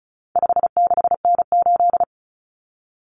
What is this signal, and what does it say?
Morse code, heard as a keyed tone, about 34 wpm, 700 Hz. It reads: 56N8
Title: Common Cut Numbers - 4-Digit 35wpm